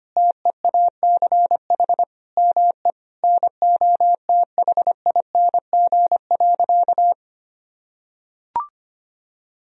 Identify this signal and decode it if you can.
Morse code, heard as a keyed tone, about 25 wpm, 700 Hz. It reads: TEACH ME NOTHING.
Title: Sentences from Top 1000 Words - Review Part 2 - 25wpm